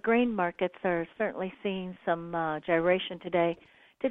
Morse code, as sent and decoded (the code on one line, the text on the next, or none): none